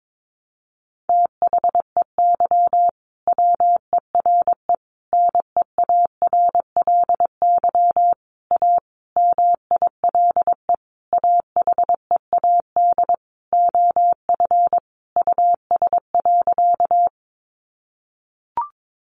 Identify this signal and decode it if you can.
Morse code, heard as a keyed tone, about 22 wpm, 700 Hz. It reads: THEY WERE NEARLY A MILE AHEAD OF US.
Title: Sentences from Top 1300 Words 22wpm